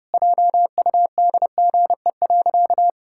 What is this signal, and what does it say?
Morse code, heard as a keyed tone, about 30 wpm, 700 Hz. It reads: JUDGE.
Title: Pangram Challenge v2 - 30wpm